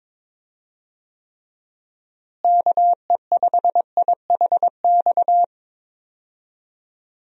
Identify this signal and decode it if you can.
Morse code, heard as a keyed tone, about 22 wpm, 700 Hz. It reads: KE5IHX
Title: USA Call Signs - 6 Characters - Repeated 3x - 22wpm